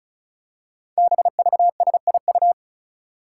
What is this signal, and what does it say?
Morse code, heard as a keyed tone, about 36 wpm, 700 Hz. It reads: BVSIU
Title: Five Challenging Letter-Number Combinations 35wpm